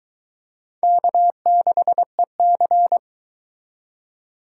Morse code, heard as a keyed tone, about 24 wpm, 700 Hz. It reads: K6EC